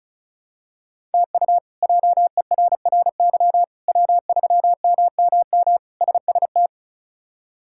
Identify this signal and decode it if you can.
Morse code, heard as a keyed tone, about 36 wpm, 700 Hz. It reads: TU JERRY W3MMM SST
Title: K1USN SST QSO Practice - 35wpm